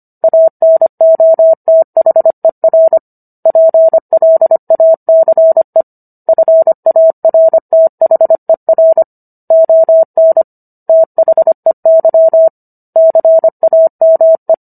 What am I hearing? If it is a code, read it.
Morse code, heard as a keyed tone, about 25 wpm, 650 Hz. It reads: ANOTHER PLACE FARTHER ON THEY CAME